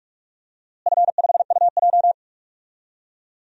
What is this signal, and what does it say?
Morse code, heard as a keyed tone, about 46 wpm, 700 Hz. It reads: F5UJ
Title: DXCC-100 Call Signs - 4 Characters - Repeated 2x - 45wpm